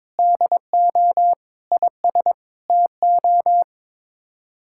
Morse code, heard as a keyed tone, about 22 wpm, 700 Hz. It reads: DO IS TO
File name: practice / 22wpm / Sets of 7 Words - 22wpm